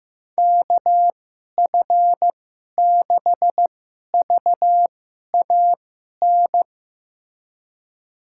Morse code, heard as a keyed tone, about 15 wpm, spaced as for 11 wpm, 700 Hz. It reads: KF6VAN
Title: USA Call Signs - 6 Characters - 2x Character Spacing - 15wpm